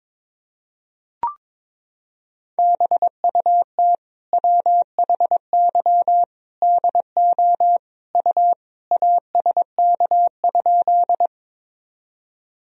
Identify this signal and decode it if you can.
Morse code, heard as a keyed tone, about 22 wpm, 700 Hz. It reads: BUT WHY DO U ASK?
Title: Sentences from Top 200 Words and Abbreviations 22wpm